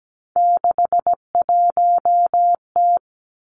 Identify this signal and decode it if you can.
Morse code, heard as a keyed tone, about 17 wpm, 700 Hz. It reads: 61T
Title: Three Challenging Letter-Number Combinations 17wpm